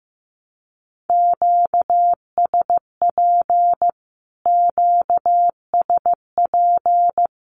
Morse code, heard as a keyed tone, about 14 wpm, 700 Hz. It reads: QSP QSP
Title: Q-Codes Repeated 3x 15wpm